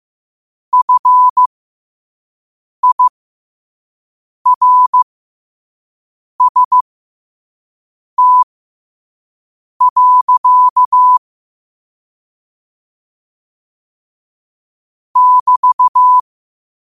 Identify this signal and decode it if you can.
Morse code, heard as a keyed tone, about 15 wpm, spaced as for 5 wpm, 1000 Hz. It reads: FIRST. <BT>
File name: qotdfiles / 5wpm